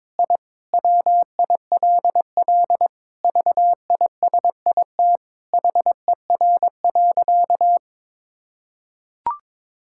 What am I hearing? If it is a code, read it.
Morse code, heard as a keyed tone, about 22 wpm, 700 Hz. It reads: I WILL VISIT HER.
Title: Sentences from Top 1000 Words - Review Part 6 - 22wpm